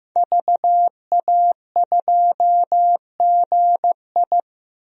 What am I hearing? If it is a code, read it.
Morse code, heard as a keyed tone, about 15 wpm, 700 Hz. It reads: VA2GI
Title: DXCC-100 Call Signs - 5 Characters - Repeated 2x - 15wpm